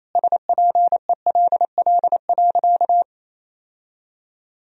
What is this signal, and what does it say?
Morse code, heard as a keyed tone, about 27 wpm, 700 Hz. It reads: SPELL.
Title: Sentences from Top 1000 Words - Review Part 1 - 28wpm